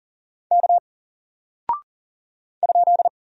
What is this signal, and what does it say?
Morse code, heard as a keyed tone, about 40 wpm, 700 Hz. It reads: K ?
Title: Single Character (Letter:Number:Punctuation:Prosign) Alt 40wpm